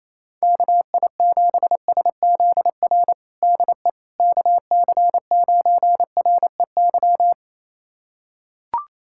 Morse code, heard as a keyed tone, about 28 wpm, 700 Hz. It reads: KI7SZR DE KC9REY
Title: USA Callsign DE Callsign - 6 Characters - 28wpm